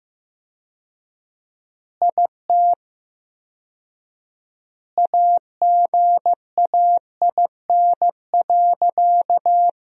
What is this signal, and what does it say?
Morse code, heard as a keyed tone, about 15 wpm, 700 Hz. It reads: IT AGAIN.
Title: 4x Word-Spacing - Sentences from Top 300 Words - 15wpm